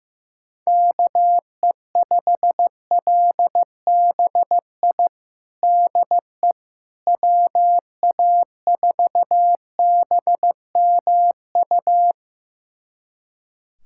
Morse code, heard as a keyed tone, about 16 wpm, 700 Hz. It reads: KE5LBI DE WA4BMU
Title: USA Callsign DE Callsign - 6 Characters - 15wpm